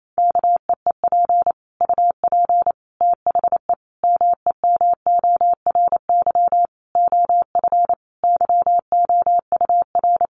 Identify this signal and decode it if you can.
Morse code, heard as a keyed tone, about 28 wpm, 700 Hz. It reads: KEEP UP THE MEMORY OF YOUR
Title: Sentences from Top 1300 Words 28wpm